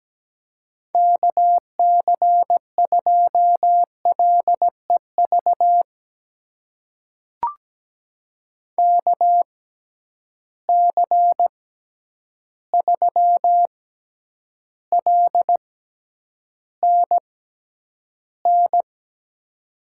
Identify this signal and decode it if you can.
Morse code, heard as a keyed tone, about 17 wpm, spaced as for 9 wpm, 700 Hz. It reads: KC2LEV KC3LNN
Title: USA Call Signs - 6 Characters - 6x Character Spacing - 17wpm